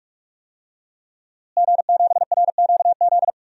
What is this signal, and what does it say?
Morse code, heard as a keyed tone, about 41 wpm, 700 Hz. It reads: G7RQZ
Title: DXCC-100 Call Signs - 5 Characters - Repeated 2x - 45wpm